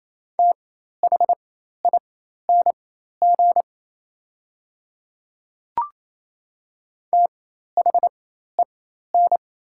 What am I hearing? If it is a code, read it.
Morse code, heard as a keyed tone, about 28 wpm, spaced as for 13 wpm, 700 Hz. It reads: THING THEN
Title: 4x Character Spacing - Top 100 Words - 28wpm